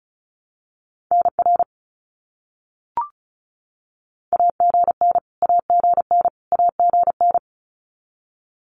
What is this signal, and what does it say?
Morse code, heard as a keyed tone, about 34 wpm, 700 Hz. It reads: NR AGN AGN AGN
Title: Abbreviations Repeated 3x - 35wpm